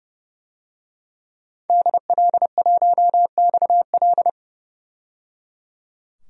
Morse code, heard as a keyed tone, about 29 wpm, 700 Hz. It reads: DL1XL
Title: DXCC-100 Call Signs - 5 Characters - 30wpm